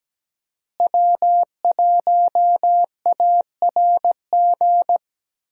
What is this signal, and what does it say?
Morse code, heard as a keyed tone, about 17 wpm, 700 Hz. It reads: W1ARG
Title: USA Call Signs - 5 Characters - Repeated 2x - 17wpm